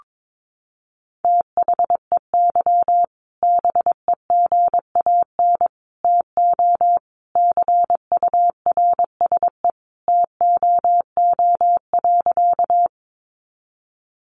Morse code, heard as a keyed tone, about 22 wpm, 700 Hz. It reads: THEY BEGAN TO CURSE TOO.
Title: A Sound Of Thunder 22wpm